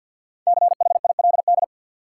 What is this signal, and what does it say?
Morse code, heard as a keyed tone, about 53 wpm, 700 Hz. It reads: CHILD
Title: Five Letter Words - 50wpm